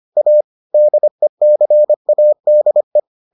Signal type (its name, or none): Morse code